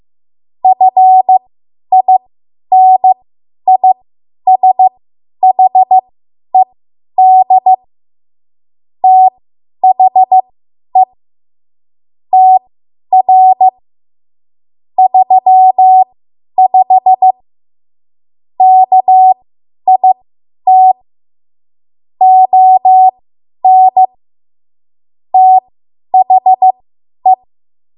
Morse code, heard as a keyed tone, about 15 wpm, spaced as for 10 wpm, 750 Hz. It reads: FINISHED THE TR 35 KIT ON THE
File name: Morse / 10WPM